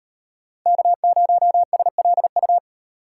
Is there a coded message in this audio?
Yes: Morse code from 0.7 to 2.6 s, heard as a keyed tone, about 36 wpm, 700 Hz. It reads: K0SLU